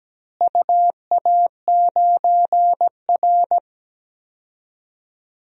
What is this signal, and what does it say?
Morse code, heard as a keyed tone, about 17 wpm, 700 Hz. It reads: UA9R
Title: DXCC-100 Call Signs - 4 Characters - Repeated 3x - 17wpm